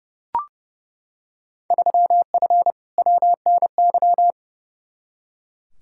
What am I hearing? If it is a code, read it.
Morse code, heard as a keyed tone, about 30 wpm, 700 Hz. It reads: 3F WNY
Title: Field Day Exchanges - 30wpm